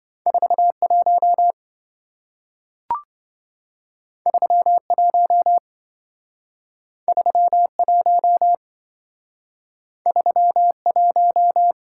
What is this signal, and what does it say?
Morse code, heard as a keyed tone, about 27 wpm, 700 Hz. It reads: 41 31 31 31